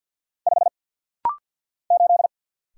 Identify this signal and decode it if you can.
Morse code, heard as a keyed tone, about 50 wpm, 700 Hz. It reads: 5 8
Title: Single Letter-Number - Rapid-Fire 50wpm